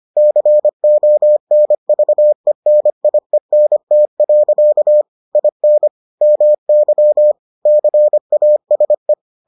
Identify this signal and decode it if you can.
Morse code, heard as a keyed tone, about 25 wpm, 600 Hz. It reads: CONVENIENT. IN MY CASE